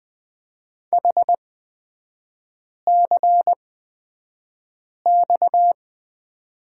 Morse code, heard as a keyed tone, about 20 wpm, spaced as for 5 wpm, 700 Hz. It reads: HCX
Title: 243 Lesson 39e - Letter z - Three Characters 20wpm